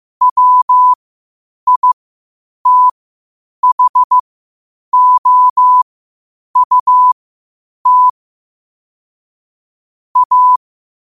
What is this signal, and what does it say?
Morse code, heard as a keyed tone, about 15 wpm, spaced as for 8 wpm, 1000 Hz. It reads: WITHOUT A
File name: 8wpm